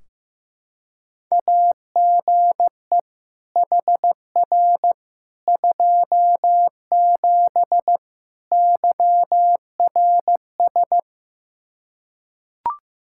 Morse code, heard as a keyed tone, about 15 wpm, 700 Hz. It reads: AGE HR 27 YRS
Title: Practice QSO - Exchange 3 - More Info - 15wpm